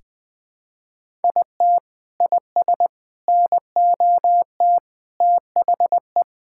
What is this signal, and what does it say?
Morse code, heard as a keyed tone, about 20 wpm, 700 Hz. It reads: IT IS NOT THE
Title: Sentences from Top 1000 Words - Review Part 5 - 20wpm